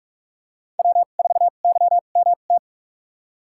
Morse code, heard as a keyed tone, about 43 wpm, spaced as for 32 wpm, 700 Hz. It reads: W4YMT
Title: USA Call Signs - 5 Characters - 2x Character Spacing - 45wpm